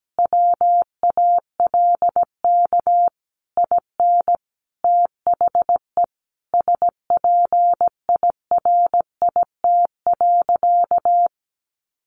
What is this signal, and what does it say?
Morse code, heard as a keyed tone, about 17 wpm, 700 Hz. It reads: WALK IN THE SPIRIT.
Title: Sentences from Top 1400 Words 17wpm